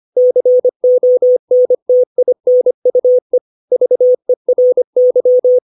Morse code, heard as a keyed tone, about 25 wpm, 500 Hz. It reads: CONTINUE VERY
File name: 632 words / 632 Words 25WPM